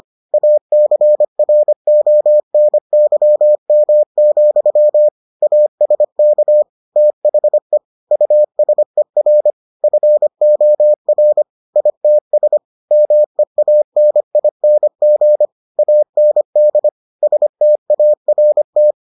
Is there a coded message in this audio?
Yes: Morse code from 0.3 to 18.9 s, heard as a keyed tone, about 25 wpm, 600 Hz. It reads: ACRONYM, ASK THE USER FOR ITS MEANING AND START